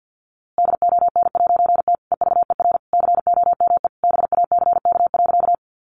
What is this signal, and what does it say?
Morse code, heard as a keyed tone, about 50 wpm, 700 Hz. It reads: DON'T EVER COME BACK.
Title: A Sound Of Thunder 50wpm